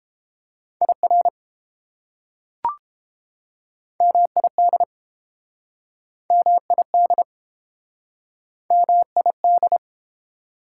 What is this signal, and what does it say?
Morse code, heard as a keyed tone, about 32 wpm, 700 Hz. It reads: IR MID MID MID